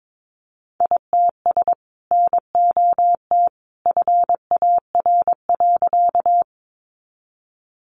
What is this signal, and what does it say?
Morse code, heard as a keyed tone, about 22 wpm, 700 Hz. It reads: ITS NOT FAR.
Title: Sentences from Top 600 Words 22wpm